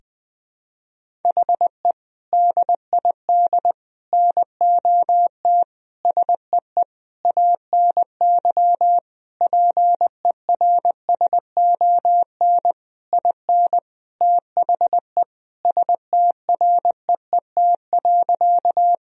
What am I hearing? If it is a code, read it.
Morse code, heard as a keyed tone, about 20 wpm, 700 Hz. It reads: HE DID NOT SEE ANY PERSON IN THE STREET.
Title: Sentences from Top 1000 Words - Review Part 3 - 20wpm